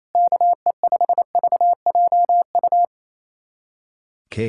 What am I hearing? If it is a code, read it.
Morse code, heard as a keyed tone, about 28 wpm, 700 Hz. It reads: KE5VJU
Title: USA Callsign DE Callsign 28wpm